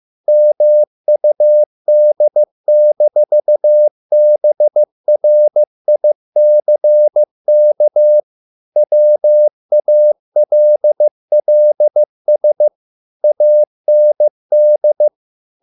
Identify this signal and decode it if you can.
Morse code, heard as a keyed tone, about 15 wpm, 600 Hz. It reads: MUD-BRICK WALLS AND